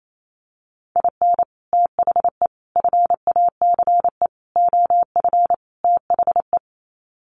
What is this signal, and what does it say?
Morse code, heard as a keyed tone, about 28 wpm, 700 Hz. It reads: IN THE FACE OF THE
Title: Sets of 5 Words 28wpm